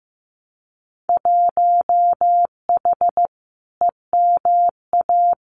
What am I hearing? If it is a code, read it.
Morse code, heard as a keyed tone, about 15 wpm, 700 Hz. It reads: 1H EMA
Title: Winter Field Day Exchanges 15wpm